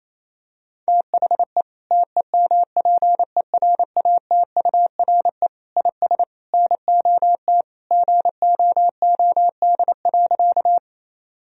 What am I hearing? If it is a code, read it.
Morse code, heard as a keyed tone, about 28 wpm, 700 Hz. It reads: THE TEMPERATURE IS NOT GOOD.
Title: Sentences from Top 1000 Words - Review Part 4 - 28wpm